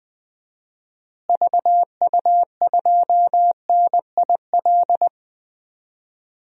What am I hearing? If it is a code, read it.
Morse code, heard as a keyed tone, about 21 wpm, 700 Hz. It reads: VU2NIL
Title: DXCC-100 Call Signs - 6 Characters - Repeated 2x - 20wpm